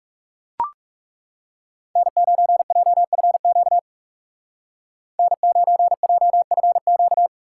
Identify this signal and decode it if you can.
Morse code, heard as a keyed tone, about 36 wpm, 700 Hz. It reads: N9JFQ N9JFQ
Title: Speed Racing - USA 1x3 Call Signs 30wpm